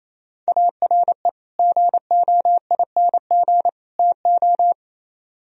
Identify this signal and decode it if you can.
Morse code, heard as a keyed tone, about 34 wpm, 700 Hz. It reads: ARE GOING TO
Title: Speed-Racing - Sets of 3 Words 28wpm